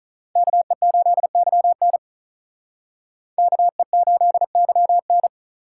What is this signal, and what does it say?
Morse code, heard as a keyed tone, about 47 wpm, 700 Hz. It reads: KE8YN KE8YN